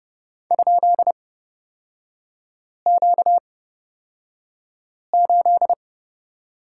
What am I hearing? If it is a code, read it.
Morse code, heard as a keyed tone, about 30 wpm, spaced as for 5 wpm, 700 Hz. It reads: ?Q8
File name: Lesson 38e - Letter q - Three Characters 30wpm